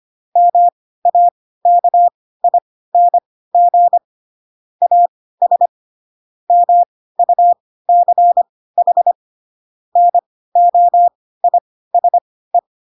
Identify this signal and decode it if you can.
Morse code, heard as a keyed tone, about 25 wpm, spaced as for 16 wpm, 700 Hz. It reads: MAKING AS MUCH NOISE
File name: Pinocchio / Pinocchio ch3 10 25 ik1hge